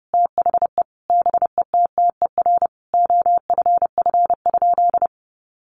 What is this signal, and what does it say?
Morse code, heard as a keyed tone, about 30 wpm, 700 Hz. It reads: THE BETTER OFF?
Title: Sentences from Top 500 Words 30wpm